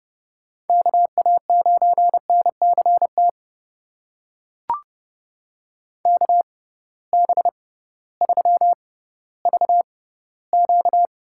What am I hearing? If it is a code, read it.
Morse code, heard as a keyed tone, about 30 wpm, spaced as for 16 wpm, 700 Hz. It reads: KA9NCT KB3VQ
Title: USA Call Signs - 6 Characters - 6x Character Spacing - 30wpm